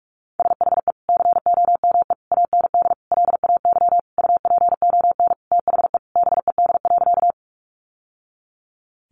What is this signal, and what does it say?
Morse code, heard as a keyed tone, about 44 wpm, 700 Hz. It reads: SHE COME AND LAY UPON THE BED.
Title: Sentences from Top 1100 Words 45wpm